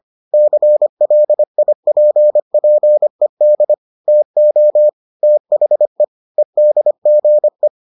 Morse code, heard as a keyed tone, about 25 wpm, 600 Hz. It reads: CLIPPED TO THE EDGE